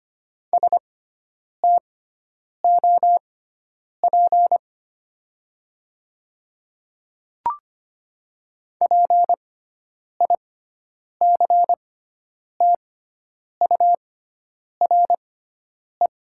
Morse code, heard as a keyed tone, about 25 wpm, spaced as for 9 wpm, 700 Hz. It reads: STOP PICTURE
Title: 6x Character Spacing - Top 300 Words - 25wpm